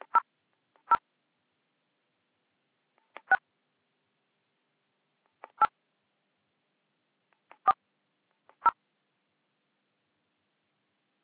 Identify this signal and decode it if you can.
DTMF, heard as telephone touch tones, tones of about 60 ms. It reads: 0 6 8 4 0